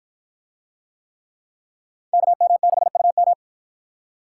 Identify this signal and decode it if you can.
Morse code, heard as a keyed tone, about 49 wpm, 700 Hz. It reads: KM6UM